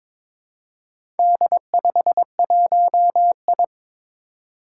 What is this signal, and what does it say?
Morse code, heard as a keyed tone, about 22 wpm, 700 Hz. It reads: D51I
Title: Four Challenging Letter-Number Combinations 22wpm